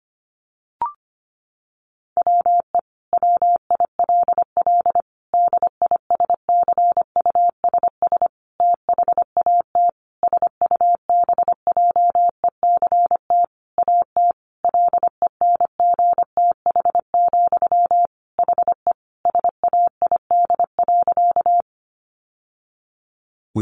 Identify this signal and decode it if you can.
Morse code, heard as a keyed tone, about 25 wpm, 700 Hz. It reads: WE WILL DISCUSS THAT SUBJECT AT LENGTH, HE SAID.
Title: Sentences from Top 1000 Words 25wpm